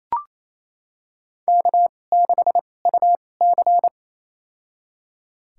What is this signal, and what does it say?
Morse code, heard as a keyed tone, about 28 wpm, spaced as for 20 wpm, 700 Hz. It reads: K6UC
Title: USA Call Signs - 4 Characters - 2x Character Spacing - 28wpm